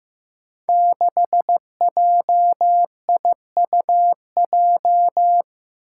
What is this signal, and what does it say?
Morse code, heard as a keyed tone, about 15 wpm, 700 Hz. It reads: 6JIUJ
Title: Five Challenging Letter-Number Combinations 15wpm